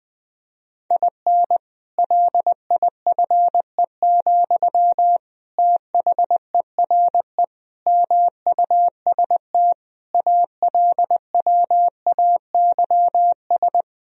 Morse code, heard as a keyed tone, about 20 wpm, 700 Hz. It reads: IN LIFE, THERE MUST ALWAYS
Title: Sentences from Top 1000 Words - Review Part 4 - 20wpm